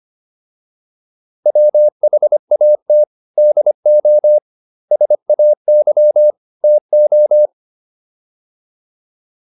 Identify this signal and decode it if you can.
Morse code, heard as a keyed tone, about 25 wpm, 600 Hz. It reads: WHAT DO SAY TO